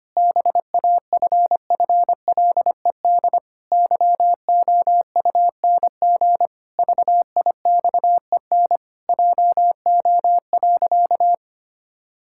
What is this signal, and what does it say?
Morse code, heard as a keyed tone, about 25 wpm, 700 Hz. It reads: BAFFLED YOUNG VIXEN JO.
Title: Pangram Challenge v2 - 25wpm